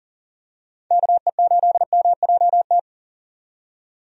Morse code, heard as a keyed tone, about 40 wpm, 700 Hz. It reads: KE8MJT
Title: USA Call Signs - 6 Characters - 40wpm